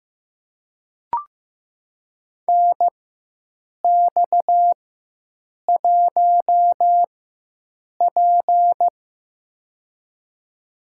Morse code, heard as a keyed tone, about 15 wpm, spaced as for 7 wpm, 700 Hz. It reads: NX1P
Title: USA Call Signs - 4 Characters - 4x Character Spacing - 15wpm